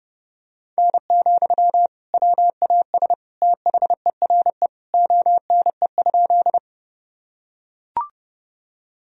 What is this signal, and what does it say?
Morse code, heard as a keyed tone, about 30 wpm, 700 Hz. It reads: N, WAS THERE ONE?
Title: Sentences from Top 100 Words and Abbreviations 30wpm